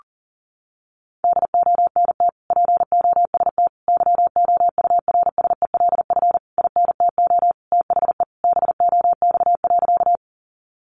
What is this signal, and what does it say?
Morse code, heard as a keyed tone, about 40 wpm, 700 Hz. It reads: DONT POST YOURSELF INTO THE BOX.
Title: Sentences from Top 1000 Words 40wpm